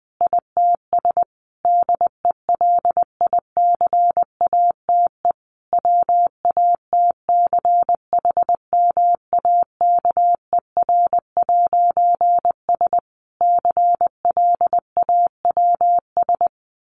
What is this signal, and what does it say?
Morse code, heard as a keyed tone, about 19 wpm, 700 Hz. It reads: ITS DELICATE WATCHMAKER'S CLAWS
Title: A Sound Of Thunder 20wpm